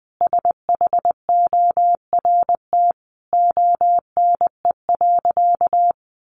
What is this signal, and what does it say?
Morse code, heard as a keyed tone, about 19 wpm, 700 Hz. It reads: SHORT ONE.
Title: Sentences from Top 1100 Words 20wpm